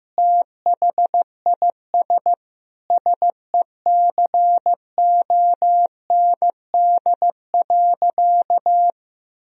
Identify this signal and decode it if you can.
Morse code, heard as a keyed tone, about 15 wpm, 700 Hz. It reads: THIS SECOND.
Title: Sentences from Top 1000 Words - Review Part 1 - 15wpm